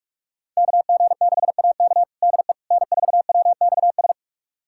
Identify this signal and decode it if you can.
Morse code, heard as a keyed tone, about 41 wpm, 700 Hz. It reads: KG6AK DE N4WXS
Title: USA Callsign DE Callsign - 5 Characters - 45wpm